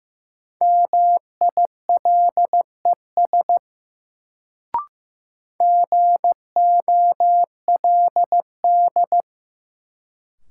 Morse code, heard as a keyed tone, about 15 wpm, 700 Hz. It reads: MILES GOLD
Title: Sentences from Top 1000 Words - Review Part 3 - New Word Drill - 15wpm